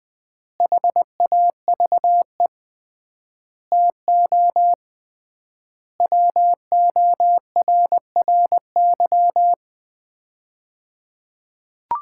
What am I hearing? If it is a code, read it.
Morse code, heard as a keyed tone, about 20 wpm, 700 Hz. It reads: HAVE TO WORRY
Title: 3x Word-Spacing - Sets of 5 Words - 20wpm